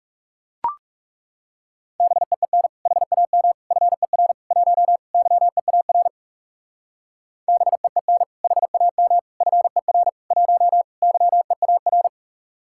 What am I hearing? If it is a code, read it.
Morse code, heard as a keyed tone, about 40 wpm, 700 Hz. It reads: BEEN HAM FER 1 YEAR BEEN HAM FER 1 YEAR